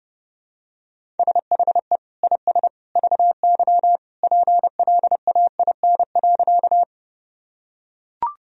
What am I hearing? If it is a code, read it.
Morse code, heard as a keyed tone, about 30 wpm, 700 Hz. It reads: SHE IS VY PLAIN.
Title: Sentences from Top 400 Words and Abbreviations 30wpm